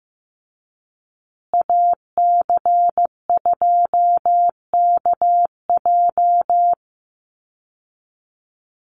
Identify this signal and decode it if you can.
Morse code, heard as a keyed tone, about 14 wpm, 700 Hz. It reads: AC2KJ
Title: Call Signs Repeated 2x - 15wpm